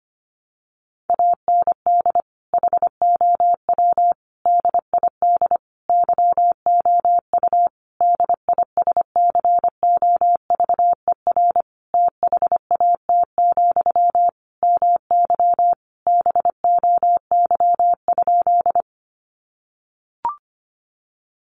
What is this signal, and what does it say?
Morse code, heard as a keyed tone, about 25 wpm, 700 Hz. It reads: AND HOW DID YOU DISCOVER THAT, MY BOY?
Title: Sentences from Top 1100 Words 25wpm